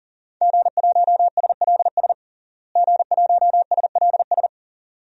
Morse code, heard as a keyed tone, about 41 wpm, 700 Hz. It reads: G1SLS G1SLS
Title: DXCC-100 Call Signs - 5 Characters - Repeated 2x - 40wpm